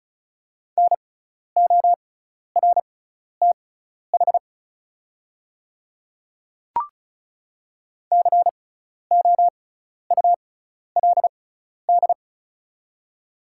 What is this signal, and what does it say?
Morse code, heard as a keyed tone, about 36 wpm, spaced as for 12 wpm, 700 Hz. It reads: NORTH COULD
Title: 6x Character Spacing - Top 300 Words - 35wpm